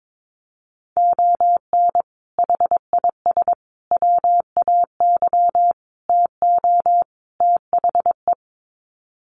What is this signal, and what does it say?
Morse code, heard as a keyed tone, about 22 wpm, 700 Hz. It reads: ON HIS WAY TO THE
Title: Sets of 5 Words Encore2 22wpm